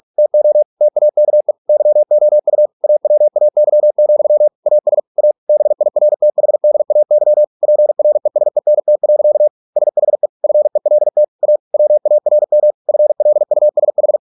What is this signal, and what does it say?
Morse code, heard as a keyed tone, about 46 wpm, 600 Hz. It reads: TO TAKE YOU AWAY, AS A BIRTHDAY PRESENT. SHE FELT A WARM FLUSH